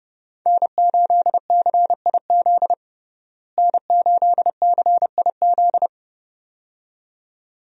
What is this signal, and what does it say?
Morse code, heard as a keyed tone, about 30 wpm, 700 Hz. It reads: N8CIZ N8CIZ
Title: USA Call Signs - 5 Characters - Repeated 2x - 30wpm